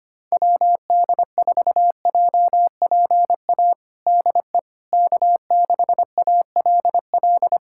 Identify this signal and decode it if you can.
Morse code, heard as a keyed tone, about 25 wpm, 700 Hz. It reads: WD4JPA DE K6ALL
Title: USA Callsign DE Callsign 25wpm